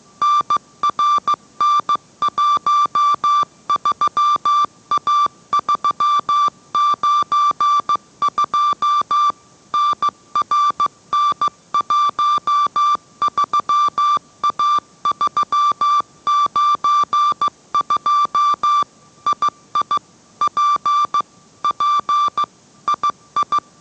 Morse code, heard as a keyed tone, about 15 wpm, 1170 Hz. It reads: NRN13A392 NRN13A392 II P P II